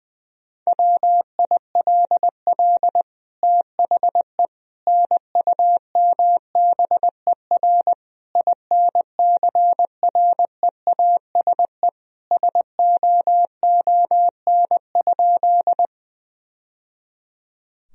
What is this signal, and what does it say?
Morse code, heard as a keyed tone, about 20 wpm, 700 Hz. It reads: WILL THE NUMBER INCREASE SOON?
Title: Sentences from Top 1000 Words - Review Part 4 - 20wpm